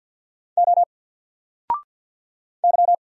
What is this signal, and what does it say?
Morse code, heard as a keyed tone, about 50 wpm, 700 Hz. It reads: O Y
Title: Single Letter-Number 50wpm